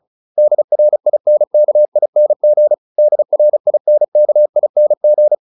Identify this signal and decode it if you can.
Morse code, heard as a keyed tone, about 34 wpm, 600 Hz. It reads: DRINKING DRINKING